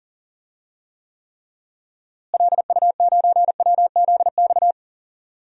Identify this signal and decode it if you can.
Morse code, heard as a keyed tone, about 40 wpm, 700 Hz. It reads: LU9WZX
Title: DXCC-100 Call Signs - 6 Characters - Repeated 3x - 40wpm